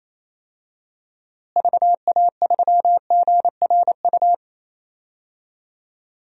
Morse code, heard as a keyed tone, about 27 wpm, 700 Hz. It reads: VA3GRU
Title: DXCC-100 Call Signs - 6 Characters - Repeated 2x - 28wpm